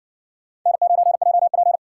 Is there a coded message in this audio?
Yes: Morse code from 0.7 to 1.8 s, heard as a keyed tone, about 55 wpm, 700 Hz. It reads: N9JP